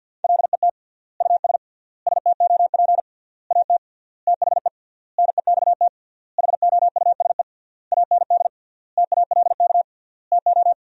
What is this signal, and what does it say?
Morse code, heard as a keyed tone, about 50 wpm, 700 Hz. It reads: LET US STOP AT THE NEXT HOUSE AND TALK TO
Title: 3x Word-Spacing - Sentences from Top 400 Words - 50wpm